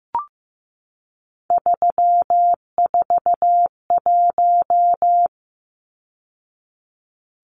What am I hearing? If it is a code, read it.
Morse code, heard as a keyed tone, about 15 wpm, 700 Hz. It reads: 341
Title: Numbers - 3-Digit 15wpm